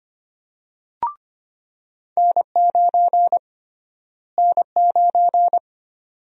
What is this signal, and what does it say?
Morse code, heard as a keyed tone, about 25 wpm, 700 Hz. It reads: N9 N9
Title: USA Call Sign Prefixes - Repeated 2x - 25wpm